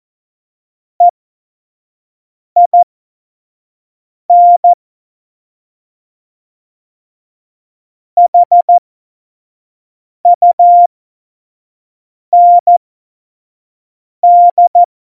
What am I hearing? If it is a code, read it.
Morse code, heard as a keyed tone, about 14 wpm, spaced as for 5 wpm, 700 Hz. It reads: EIN HUND
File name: Morsekurs / Witze / Witz 18. Dezember 2018, 25 F70